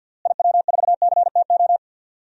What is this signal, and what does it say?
Morse code, heard as a keyed tone, about 46 wpm, 700 Hz. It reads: IW4CTO
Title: DXCC-100 Call Signs - 6 Characters - Repeated 3x - 50wpm